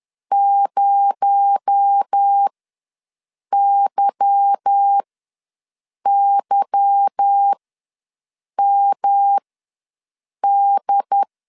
Morse code, heard as a keyed tone, about 10 wpm, spaced as for 6 wpm, 800 Hz. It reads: 0YYMD